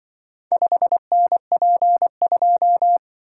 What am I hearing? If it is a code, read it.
Morse code, heard as a keyed tone, about 28 wpm, 700 Hz. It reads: 5NP2